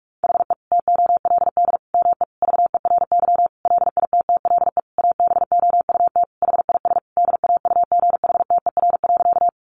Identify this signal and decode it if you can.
Morse code, heard as a keyed tone, about 44 wpm, 700 Hz. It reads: HE TOLD ME VERY LITTLE ABOUT HIS DAUGHTER.
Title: Sentences from Top 1500 Words 45wpm